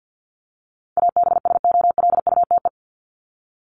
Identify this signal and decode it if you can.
Morse code, heard as a keyed tone, about 50 wpm, 700 Hz. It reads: ABSOLUTE
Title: LY Suffix - Top 100 Words with Base Words 50wpm